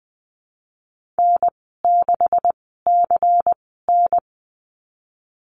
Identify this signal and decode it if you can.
Morse code, heard as a keyed tone, about 20 wpm, spaced as for 15 wpm, 700 Hz. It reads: N6CN
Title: 167 Lesson 27g - Number 7 - Call Signs 20wpm